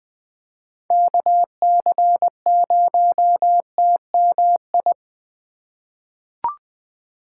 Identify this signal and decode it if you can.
Morse code, heard as a keyed tone, about 20 wpm, 700 Hz. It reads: KC0TMI
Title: USA Call Signs - 6 Characters - 2x Character Spacing - 20wpm